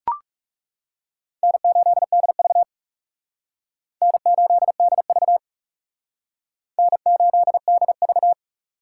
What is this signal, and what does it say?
Morse code, heard as a keyed tone, about 37 wpm, 700 Hz. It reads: N8DV N8DV N8DV
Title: Speed Racing - USA 1x2 Call Signs 30wpm